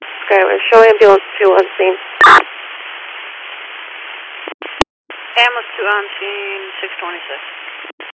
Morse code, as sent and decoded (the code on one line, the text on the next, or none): none